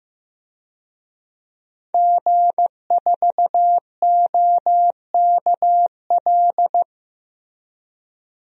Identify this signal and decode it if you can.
Morse code, heard as a keyed tone, about 16 wpm, 700 Hz. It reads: G4OKL
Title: DXCC-100 Call Signs - 5 Characters - Repeated 3x - 15wpm